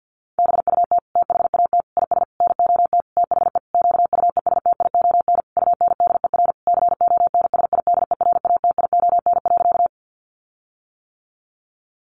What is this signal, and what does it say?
Morse code, heard as a keyed tone, about 50 wpm, 700 Hz. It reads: BUT THAT IS NOT THE QUESTION UNDER CONSIDERATION.
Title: Sentences from Top 1500 Words 50wpm